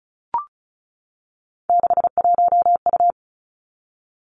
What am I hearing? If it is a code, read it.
Morse code, heard as a keyed tone, about 36 wpm, 700 Hz. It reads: 61U
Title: Three Challenging Letter-Number Combinations 35wpm